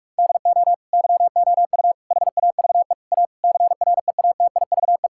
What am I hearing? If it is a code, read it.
Morse code, heard as a keyed tone, about 41 wpm, 700 Hz. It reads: DO YOU HAVE A CREATIVE
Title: Sentences from Top 1000 Words - Review Part 5 - 45wpm